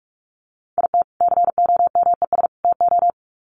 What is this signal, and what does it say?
Morse code, heard as a keyed tone, about 44 wpm, 700 Hz. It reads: IT COMES TO
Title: Sets of 3 Words 45wpm